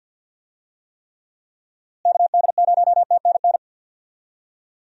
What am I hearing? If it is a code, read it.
Morse code, heard as a keyed tone, about 50 wpm, 700 Hz. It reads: KD0TNN
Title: USA Call Signs - 6 Characters - Repeated 3x - 50wpm